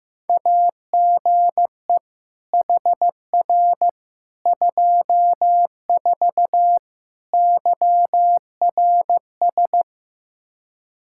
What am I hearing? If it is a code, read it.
Morse code, heard as a keyed tone, about 15 wpm, 700 Hz. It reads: AGE HR 24 YRS
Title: Practice QSO - Exchange 3 - More Info - 15wpm